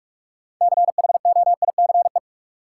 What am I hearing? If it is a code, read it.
Morse code, heard as a keyed tone, about 46 wpm, 700 Hz. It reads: CHOICE
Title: Sentences from Top 900 Words - New Word Drill - 45wpm